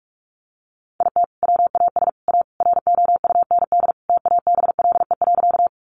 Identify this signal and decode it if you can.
Morse code, heard as a keyed tone, about 44 wpm, 700 Hz. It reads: IT WAS A ROUND TABLE.
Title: Sentences from Top 400 Words 45wpm